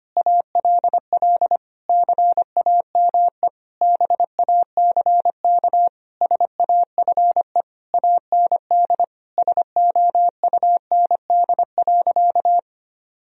Morse code, heard as a keyed tone, about 25 wpm, 700 Hz. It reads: ALL CAME BACK SAFE AND SOUND.
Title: Sentences from Top 1000 Words - Review Part 1 - 25wpm